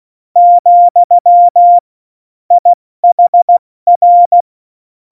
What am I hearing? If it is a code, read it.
Morse code, heard as a keyed tone, about 16 wpm, 700 Hz. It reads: , IHR
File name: Morsekurs / Witze / Witz 09. Januar 2018, 70 F80